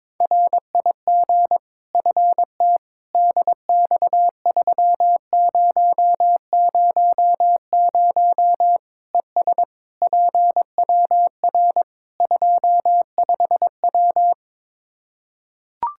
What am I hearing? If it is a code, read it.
Morse code, heard as a keyed tone, about 21 wpm, 700 Hz. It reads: RIG FT DX3000 ES PWR 25W
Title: K1USN SST QSO Practice 22wpm